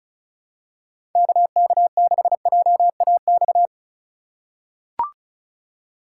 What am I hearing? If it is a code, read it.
Morse code, heard as a keyed tone, about 36 wpm, 700 Hz. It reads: KK6JAX
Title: USA Call Signs - 2x Character Spacing - 35wpm